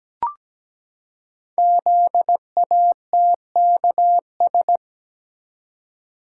Morse code, heard as a keyed tone, about 17 wpm, 700 Hz. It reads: ZATKS
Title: Code Groups - 5 Letters - 17wpm